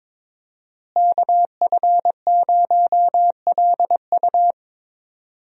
Morse code, heard as a keyed tone, about 21 wpm, 700 Hz. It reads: KF0LU
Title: USA Call Signs - 5 Characters - Repeated 3x - 22wpm